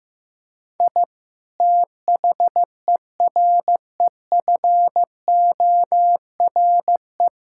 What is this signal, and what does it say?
Morse code, heard as a keyed tone, about 15 wpm, 700 Hz. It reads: I THEREFORE
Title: Sentences from Top 1000 Words - Review Part 5 - 15wpm